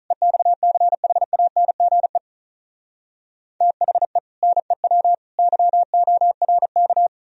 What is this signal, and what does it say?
Morse code, heard as a keyed tone, about 46 wpm, 700 Hz. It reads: EXCHANGE THE NEW YORK